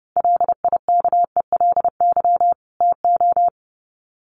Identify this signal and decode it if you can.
Morse code, heard as a keyed tone, about 30 wpm, 700 Hz. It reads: LIKELY TO
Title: Sets of 2 Words 30wpm